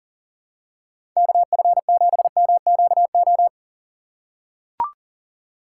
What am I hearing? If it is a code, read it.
Morse code, heard as a keyed tone, about 40 wpm, 700 Hz. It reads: KF7MQO
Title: USA Call Signs - 6 Characters - 6x Character Spacing - 40wpm